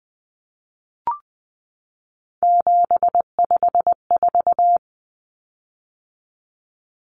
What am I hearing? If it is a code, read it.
Morse code, heard as a keyed tone, about 20 wpm, 700 Hz. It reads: 754
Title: Common Cut Numbers - 3-Digit 20wpm